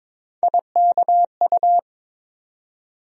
Morse code, heard as a keyed tone, about 22 wpm, 700 Hz. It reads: IKU
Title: USA Call Sign Postfixes - 3 Letters 22wpm